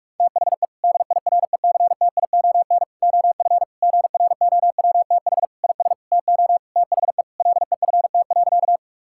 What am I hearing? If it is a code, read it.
Morse code, heard as a keyed tone, about 47 wpm, 700 Hz. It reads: THE DIRECTION OF GROWTH IS TO THE LEFT.
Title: Sentences from Top 1000 Words - Review Part 5 - 45wpm